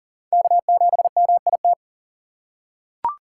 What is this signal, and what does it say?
Morse code, heard as a keyed tone, about 40 wpm, 700 Hz. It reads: K7MIT
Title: USA Call Signs - 5 Characters - 3x Character Spacing - 40wpm